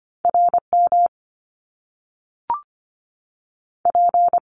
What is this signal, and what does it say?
Morse code, heard as a keyed tone, about 25 wpm, 700 Hz. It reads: RM P
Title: Top Two Letter Combinations in Words 25wpm